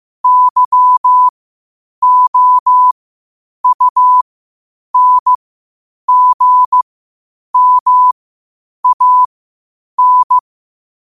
Morse code, heard as a keyed tone, about 15 wpm, spaced as for 8 wpm, 1000 Hz. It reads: YOUNGMAN
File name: qotdfiles / 8wpm